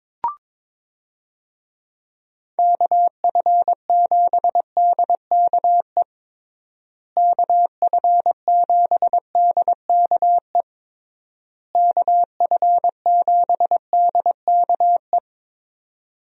Morse code, heard as a keyed tone, about 22 wpm, 700 Hz. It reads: KF7DKE KF7DKE KF7DKE
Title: USA Call Signs - 6 Characters - Repeated 3x - 22wpm